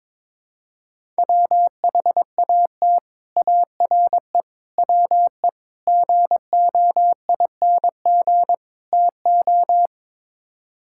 Morse code, heard as a keyed tone, about 22 wpm, 700 Hz. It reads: WHAT ARE WE GOING TO
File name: practice / 22wpm / Build A Phrase - 5-Word Phrases - 22wpm